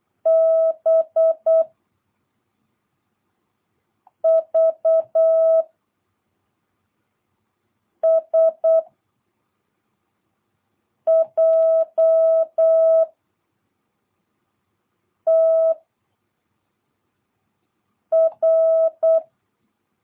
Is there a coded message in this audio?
Yes: Morse code from 0.3 to 19.2 s, heard as a keyed tone, about 8 wpm, 660 Hz. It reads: B V S J T R